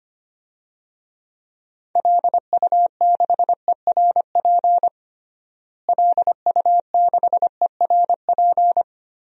Morse code, heard as a keyed tone, about 25 wpm, 700 Hz. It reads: LU6ERP LU6ERP
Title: DXCC-100 Call Signs - 6 Characters - Repeated 2x - 25wpm